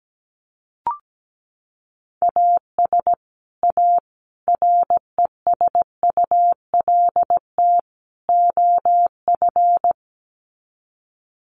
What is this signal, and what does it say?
Morse code, heard as a keyed tone, about 17 wpm, 700 Hz. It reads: AS A RESULT OF
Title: Sets of 4 Words 17wpm